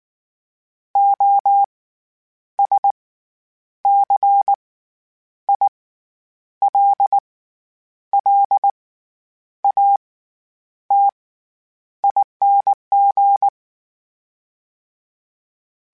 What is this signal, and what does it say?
Morse code, heard as a keyed tone, about 19 wpm, spaced as for 9 wpm, 800 Hz. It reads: OSCILLATING
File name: Special INGs / Special INGs-mp3s / SlowToFaster ing @18